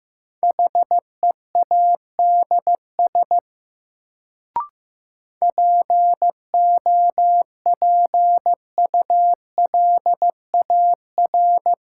Morse code, heard as a keyed tone, about 15 wpm, 700 Hz. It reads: HEADS POPULAR
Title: Sentences from Top 1000 Words - Review Part 3 - New Word Drill - 15wpm